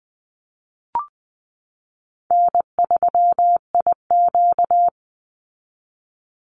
Morse code, heard as a keyed tone, about 20 wpm, 700 Hz. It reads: N3IQ
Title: Easy Call Signs Repeated 3x - 20wpm